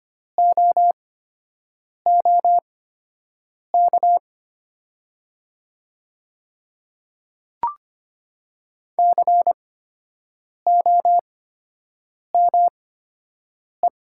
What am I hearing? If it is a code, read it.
Morse code, heard as a keyed tone, about 25 wpm, spaced as for 7 wpm, 700 Hz. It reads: OOK COME
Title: 8x Character Spacing - Top 100 Words - 25wpm